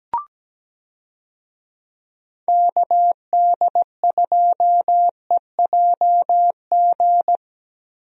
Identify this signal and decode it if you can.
Morse code, heard as a keyed tone, about 17 wpm, 700 Hz. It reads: KD2EJG
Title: USA Call Signs - 6 Characters - Repeated 3x - 17wpm